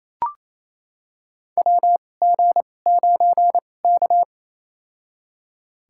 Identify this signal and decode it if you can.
Morse code, heard as a keyed tone, about 28 wpm, spaced as for 20 wpm, 700 Hz. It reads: WG9K
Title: USA Call Signs - 2x Character Spacing - 28wpm